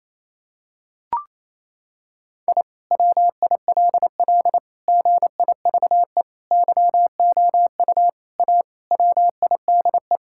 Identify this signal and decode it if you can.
Morse code, heard as a keyed tone, about 28 wpm, 700 Hz. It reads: I WILL GIVE YOU A WIDE
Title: Sentences from Top 1000 Words - Review Part 6 - 28wpm